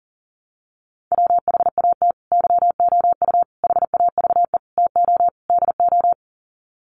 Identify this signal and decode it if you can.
Morse code, heard as a keyed tone, about 37 wpm, 700 Hz. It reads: WHAT YOU HAVE TO DO
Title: Sets of 5 Words Encore 40wpm